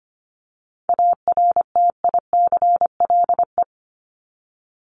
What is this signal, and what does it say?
Morse code, heard as a keyed tone, about 25 wpm, 700 Hz. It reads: ARTICLE
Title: Top 751-1000 Words 25wpm